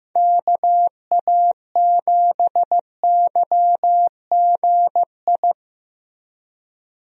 Morse code, heard as a keyed tone, about 15 wpm, 700 Hz. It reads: KA7YGI
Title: USA Call Signs - 6 Characters - Repeated 3x - 15wpm